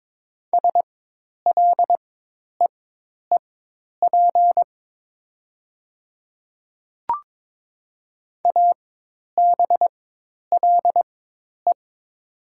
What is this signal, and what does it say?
Morse code, heard as a keyed tone, about 22 wpm, spaced as for 10 wpm, 700 Hz. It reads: SLEEP ABLE
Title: 4x Character Spacing - Top 500 Words - 22wpm